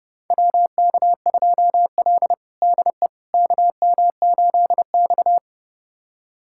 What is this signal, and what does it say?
Morse code, heard as a keyed tone, about 30 wpm, 700 Hz. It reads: WK2L DE KM8X
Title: USA Callsign DE Callsign - 4 Characters - 30wpm